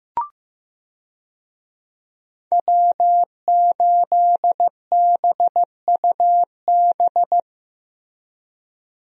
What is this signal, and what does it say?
Morse code, heard as a keyed tone, about 15 wpm, 700 Hz. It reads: W8BUB
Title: USA Call Signs - 5 Characters - Repeated 2x - 15wpm